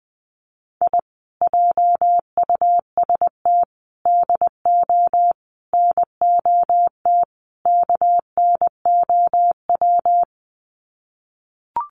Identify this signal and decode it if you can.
Morse code, heard as a keyed tone, about 20 wpm, 700 Hz. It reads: I JUST DO NOT KNOW
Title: Sets of 5 Words Encore 20wpm